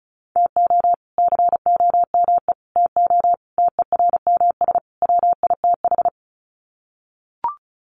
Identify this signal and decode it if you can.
Morse code, heard as a keyed tone, about 34 wpm, 700 Hz. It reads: TO COME TO TERMS WITH
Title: Sets of 5 Words Encore 35wpm